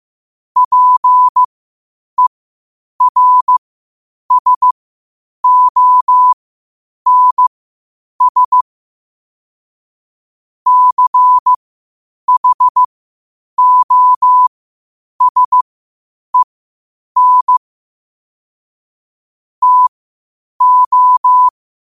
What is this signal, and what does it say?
Morse code, heard as a keyed tone, about 15 wpm, spaced as for 8 wpm, 1000 Hz. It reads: PERSONS CHOSEN TO
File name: qotdfiles / 8wpm